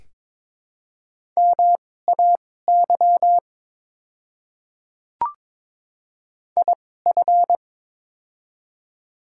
Morse code, heard as a keyed tone, about 22 wpm, spaced as for 16 wpm, 700 Hz. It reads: MAY IF
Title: 2x Character Spacing - Top 200 Words - 22wpm